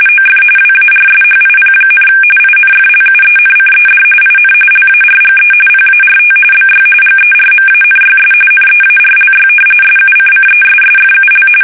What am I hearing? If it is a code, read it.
RTTY, about 45.45 baud. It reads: OF WARSHIP, INFLUENCED IN LARGE PART BY GERMANYS NEARLY SUCCESSFUL
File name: fsk850